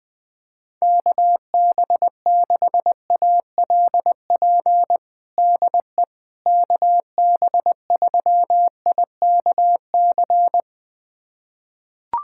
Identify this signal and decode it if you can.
Morse code, heard as a keyed tone, about 20 wpm, 700 Hz. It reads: KB6ALP DE KB3IKC
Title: USA Callsign DE Callsign 20wpm